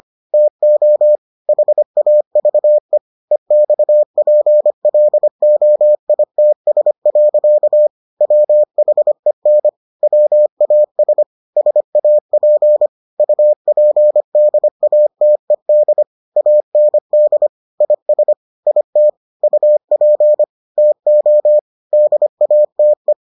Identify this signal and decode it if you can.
Morse code, heard as a keyed tone, about 25 wpm, 600 Hz. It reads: TO HAVE EXPLOITS. WHEN WAS SAP UPDATED AND IS IT UP TO DATE